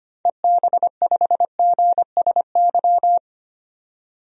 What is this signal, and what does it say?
Morse code, heard as a keyed tone, about 25 wpm, 700 Hz. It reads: EB5GSY
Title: DXCC-100 Call Signs - 6 Characters - 25wpm